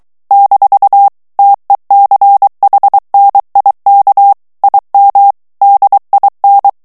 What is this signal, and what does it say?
Morse code, heard as a keyed tone, about 23 wpm, 800 Hz. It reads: - TECHNIK IM DIN
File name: ALLEMAND texte clair 1 vitesse 22 mots minute